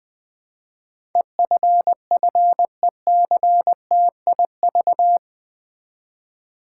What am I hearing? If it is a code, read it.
Morse code, heard as a keyed tone, about 20 wpm, 700 Hz. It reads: EFFECTIV
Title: Build A Word - 4 Letters At A Time - 11-Letter Words - 20wpm